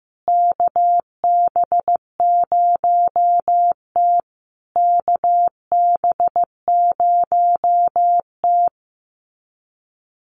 Morse code, heard as a keyed tone, about 15 wpm, 700 Hz. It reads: KB0T KB0T
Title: USA Call Signs Repeated 3x 15wpm